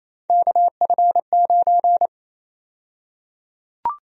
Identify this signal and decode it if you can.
Morse code, heard as a keyed tone, about 28 wpm, 700 Hz. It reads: KF9
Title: USA Call Sign Prefixes - Repeated 2x - 28wpm